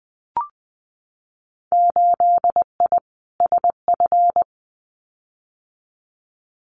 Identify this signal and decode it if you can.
Morse code, heard as a keyed tone, about 19 wpm, 700 Hz. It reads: 8I SF
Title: Winter Field Day Exchanges Encore 20wpm